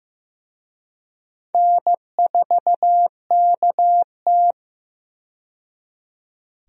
Morse code, heard as a keyed tone, about 15 wpm, 700 Hz. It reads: N4KT
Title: USA Call Signs - 4 Characters - 15wpm